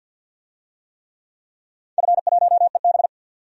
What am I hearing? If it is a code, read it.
Morse code, heard as a keyed tone, about 47 wpm, 700 Hz. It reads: F1EB
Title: DXCC-100 Call Signs - 4 Characters - 50wpm